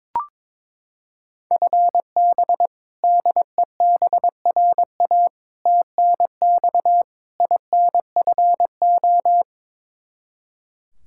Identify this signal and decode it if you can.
Morse code, heard as a keyed tone, about 22 wpm, 700 Hz. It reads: FB DEBRA TNX INFO
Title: Practice QSO - Exchange 3 - More Info - 22wpm